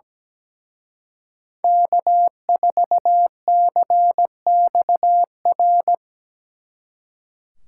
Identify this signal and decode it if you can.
Morse code, heard as a keyed tone, about 20 wpm, 700 Hz. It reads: K4CXR
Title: Speed Racing - USA 1x3 Call Signs 17wpm